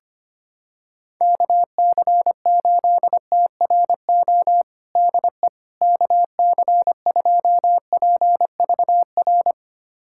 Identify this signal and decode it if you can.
Morse code, heard as a keyed tone, about 25 wpm, 700 Hz. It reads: KC8TRO DE KC2PVR
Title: USA Callsign DE Callsign - 6 Characters - 25wpm